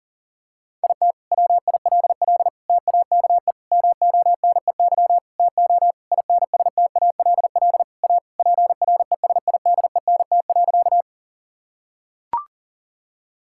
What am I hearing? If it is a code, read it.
Morse code, heard as a keyed tone, about 40 wpm, 700 Hz. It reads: IT WILL TAKE MONEY TO INSTALL A PRESIDENT.
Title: Sentences from Top 1600 Words 40wpm